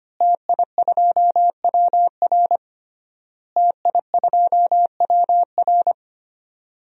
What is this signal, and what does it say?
Morse code, heard as a keyed tone, about 25 wpm, 700 Hz. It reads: TI2WR TI2WR
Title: DXCC-100 Call Signs - 5 Characters - Repeated 3x - 25wpm